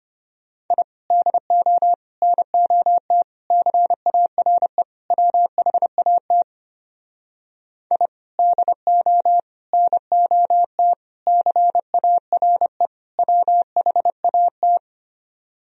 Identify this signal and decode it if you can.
Morse code, heard as a keyed tone, about 32 wpm, 700 Hz. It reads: I DO NOT CARE WHAT I DO NOT CARE WHAT